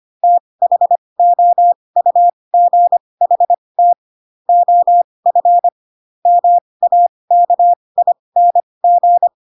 Morse code, heard as a keyed tone, about 25 wpm, spaced as for 20 wpm, 700 Hz. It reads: THOUGHT OF MAKING
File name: Pinocchio / Pinocchio ch2 15 25 ik1hge